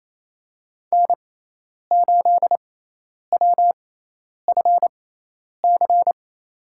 Morse code, heard as a keyed tone, about 28 wpm, spaced as for 10 wpm, 700 Hz. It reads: N8WFC
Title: USA Call Signs - 5 Characters - 6x Character Spacing - 28wpm